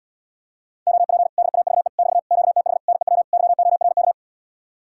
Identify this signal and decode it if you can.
Morse code, heard as a keyed tone, about 80 wpm, 700 Hz. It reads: OP DAVE <BT> QTH NEW YORK